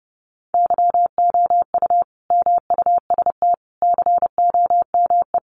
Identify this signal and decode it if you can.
Morse code, heard as a keyed tone, about 30 wpm, 700 Hz. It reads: YOU MUST COME
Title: Sentences from Top 200 Words 30wpm